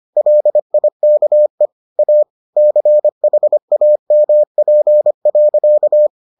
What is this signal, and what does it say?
Morse code, heard as a keyed tone, about 25 wpm, 600 Hz. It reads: LIKE A CHAMP.